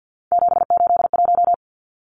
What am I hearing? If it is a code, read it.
Morse code, heard as a keyed tone, about 50 wpm, 700 Hz. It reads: 781
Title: Numbers - 3-Digit 50wpm